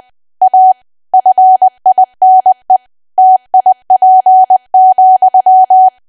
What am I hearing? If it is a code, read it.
Morse code, heard as a keyed tone, about 20 wpm, 750 Hz. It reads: A FINE TIP,
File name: cwa / W1AW-Level3 / 20WPM